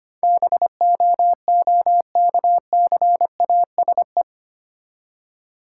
Morse code, heard as a keyed tone, about 27 wpm, 700 Hz. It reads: BOOKCASE
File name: Easy Words - Speed-Racing with Hints - 25wpm